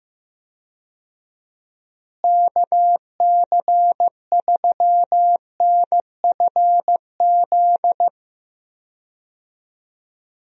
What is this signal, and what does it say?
Morse code, heard as a keyed tone, about 15 wpm, 700 Hz. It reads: KC3NFZ
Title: USA Call Signs - 6 Characters - Repeated 3x - 15wpm